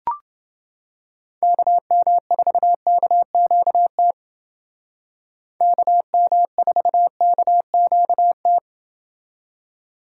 Morse code, heard as a keyed tone, about 29 wpm, 700 Hz. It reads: KM4KQT KM4KQT